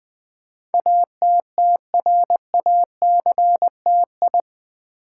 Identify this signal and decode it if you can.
Morse code, heard as a keyed tone, about 20 wpm, 700 Hz. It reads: ATTRACTI
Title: Build A Word - 4 Letters At A Time - 10-Letter Words - 20wpm